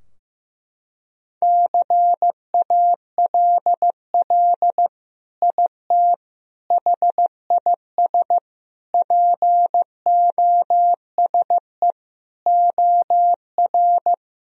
Morse code, heard as a keyed tone, about 16 wpm, 700 Hz. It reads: CALL IT HIS POSE OR
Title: Sentences from Top 1000 Words - Review Part 2 - 15wpm